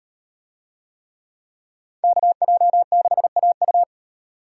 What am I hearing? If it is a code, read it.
Morse code, heard as a keyed tone, about 35 wpm, 700 Hz. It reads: KJ6AU